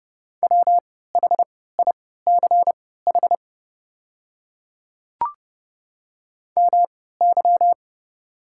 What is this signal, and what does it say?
Morse code, heard as a keyed tone, about 28 wpm, spaced as for 17 wpm, 690 Hz. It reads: WHICH MY
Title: 3x Character Spacing - Top 300 Words - 30wpm